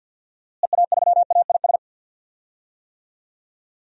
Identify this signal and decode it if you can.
Morse code, heard as a keyed tone, about 53 wpm, 700 Hz. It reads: EA3AIS
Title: DXCC-100 Call Signs - 6 Characters - Repeated 2x - 50wpm